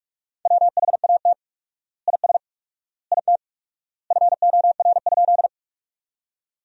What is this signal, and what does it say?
Morse code, heard as a keyed tone, about 45 wpm, 700 Hz. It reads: WHAT IS IT FOR?
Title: 4x Word-Spacing - Sentences from Top 100 Words - 45wpm